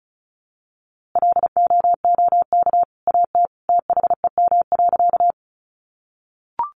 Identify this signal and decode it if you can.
Morse code, heard as a keyed tone, about 34 wpm, 700 Hz. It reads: LOOK AT THEM.
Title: A Sound Of Thunder 35wpm